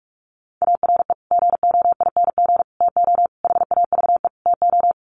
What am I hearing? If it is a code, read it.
Morse code, heard as a keyed tone, about 44 wpm, 700 Hz. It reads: ARE GOING TO HAVE TO
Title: Sets of 5 Words 45wpm